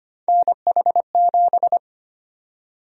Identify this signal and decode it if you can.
Morse code, heard as a keyed tone, about 25 wpm, 700 Hz. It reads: NH7
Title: USA Call Sign Prefixes - Repeated 2x - 25wpm